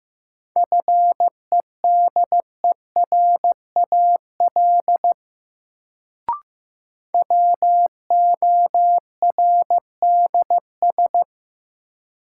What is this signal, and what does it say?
Morse code, heard as a keyed tone, about 15 wpm, 700 Hz. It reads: FEDERAL WORDS
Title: Sets of 3 Words - New Word Drill - 15wpm